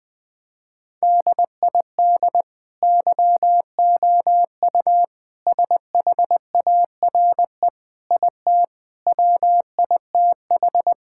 Morse code, heard as a keyed tone, about 20 wpm, 700 Hz. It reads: DID YOU SHARE IT WITH
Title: Sentences from Top 1000 Words - Review Part 6 - 20wpm